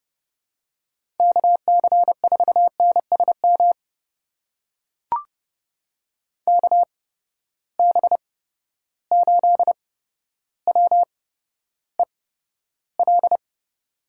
Morse code, heard as a keyed tone, about 30 wpm, spaced as for 13 wpm, 700 Hz. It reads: KC4NSM KB8WEL
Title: USA Call Signs - 6 Characters - 8x Character Spacing - 30wpm